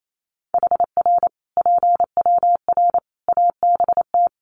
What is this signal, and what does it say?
Morse code, heard as a keyed tone, about 28 wpm, 700 Hz. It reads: HR PWR ABT
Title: Common QSO Phrases 28wpm